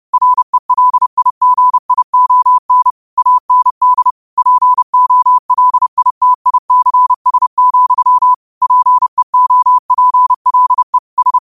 Morse code, heard as a keyed tone, about 31 wpm, 1000 Hz. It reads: RELIGION AND POLITICS, PEOPLE S
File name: qotdfiles / 30wpm